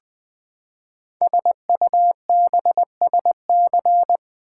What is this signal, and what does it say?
Morse code, heard as a keyed tone, about 20 wpm, 700 Hz. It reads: SUBSC
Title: Build A Word - 5 Letters At A Time - 12-Letter Words - 20wpm